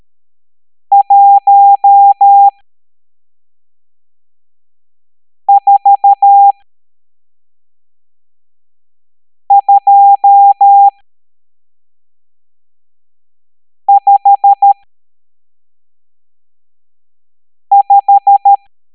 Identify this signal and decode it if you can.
Morse code, heard as a keyed tone, about 13 wpm, spaced as for 3 wpm, 800 Hz. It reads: 14255
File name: hamdata / L12-EX02-CHIFFRES